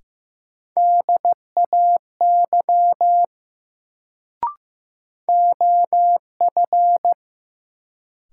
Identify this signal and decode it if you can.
Morse code, heard as a keyed tone, about 15 wpm, 700 Hz. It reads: DAY OF
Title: Sets of 2 Words - New Word Drill - 15wpm